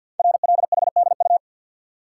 Morse code, heard as a keyed tone, about 48 wpm, 690 Hz. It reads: RLHDU
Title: Code Groups - 5 Letters - 50wpm